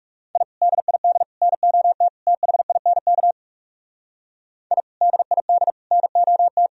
Morse code, heard as a keyed tone, about 37 wpm, 700 Hz. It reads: I DID NOT THINK I DID NOT
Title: Speed-Racing - Sets of 4 Words 30wpm